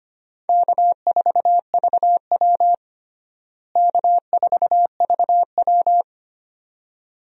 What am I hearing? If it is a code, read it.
Morse code, heard as a keyed tone, about 25 wpm, 700 Hz. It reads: K4VW K4VW
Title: USA Call Signs - 4 Characters - Repeated 2x - 25wpm